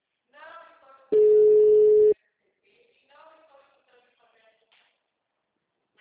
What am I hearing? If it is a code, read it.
Ringback tone (Europe, 425 Hz).